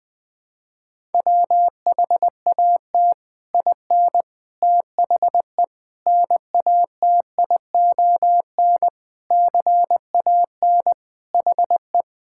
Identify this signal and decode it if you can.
Morse code, heard as a keyed tone, about 20 wpm, 700 Hz. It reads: WHAT IN THE NATION CAN HE
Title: Sentences from Top 1000 Words - Review Part 6 - 20wpm